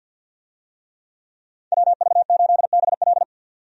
Morse code, heard as a keyed tone, about 50 wpm, 700 Hz. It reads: WV8BL
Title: USA Call Signs - 5 Characters - 50wpm